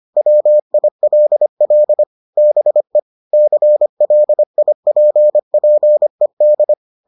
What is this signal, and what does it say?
Morse code, heard as a keyed tone, about 25 wpm, 600 Hz. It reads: WILL BE CLIPPED